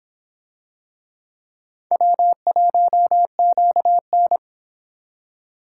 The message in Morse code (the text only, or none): W1QN